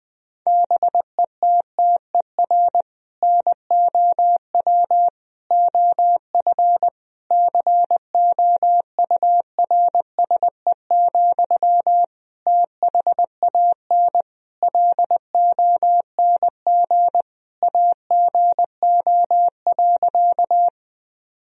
Morse code, heard as a keyed tone, about 21 wpm, 700 Hz. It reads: BETTER NOW OF COURSE, THAN LONG AGO.
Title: Sentences from Top 1000 Words - Review Part 3 - 20wpm